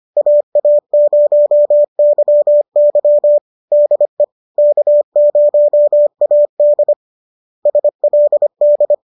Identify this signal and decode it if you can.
Morse code, heard as a keyed tone, about 25 wpm, 590 Hz. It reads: AA0YY DE K0AD SLD
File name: qso213 25